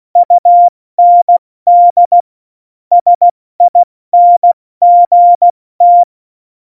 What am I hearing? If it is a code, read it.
Morse code, heard as a keyed tone, about 16 wpm, spaced as for 14 wpm, 700 Hz. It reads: UND SINGT
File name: Morsekurs / Witze / Witz 17. Oktober 17, Tempo 70